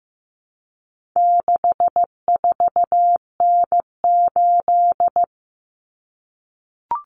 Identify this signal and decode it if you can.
Morse code, heard as a keyed tone, about 15 wpm, 700 Hz. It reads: 64N8
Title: Common Cut Numbers - 4-Digit 15wpm